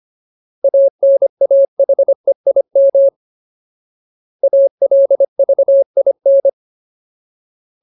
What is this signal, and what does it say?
Morse code, heard as a keyed tone, about 25 wpm, 550 Hz. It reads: ANAHEIM ALVIN
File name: Ultimate Word List / 25WPM Ultimate